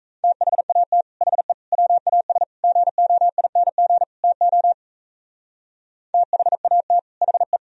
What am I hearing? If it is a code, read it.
Morse code, heard as a keyed tone, about 35 wpm, 700 Hz. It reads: THAT HE WAS GOING TO THAT HE
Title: Speed-Racing - Sets of 5 Words 28wpm